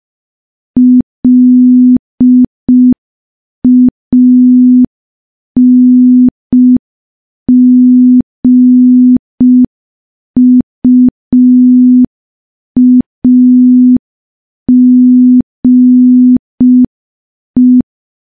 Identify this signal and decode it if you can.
Morse code, heard as a keyed tone, about 5 wpm, 250 Hz. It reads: LANGUAGE
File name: morse / morse-training-05WPM